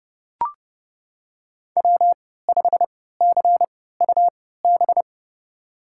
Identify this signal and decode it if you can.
Morse code, heard as a keyed tone, about 32 wpm, spaced as for 17 wpm, 700 Hz. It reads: W5CUB
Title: USA Call Signs - 5 Characters - 3x Character Spacing - 30wpm